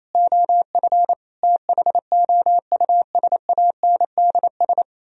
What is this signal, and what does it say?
Morse code, heard as a keyed tone, about 36 wpm, 700 Hz. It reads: OF THOUSANDS